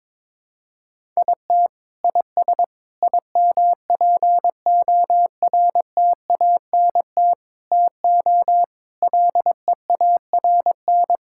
Morse code, heard as a keyed tone, about 22 wpm, 700 Hz. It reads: IT IS IMPORTANT TO LEARN
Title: Sentences from Top 1000 Words - Review Part 6 - 22wpm